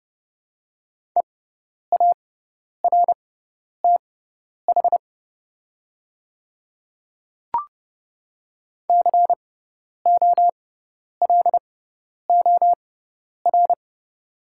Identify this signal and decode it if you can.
Morse code, heard as a keyed tone, about 30 wpm, spaced as for 10 wpm, 700 Hz. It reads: EARTH COLOR
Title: 6x Character Spacing - Top 300 Words - 30wpm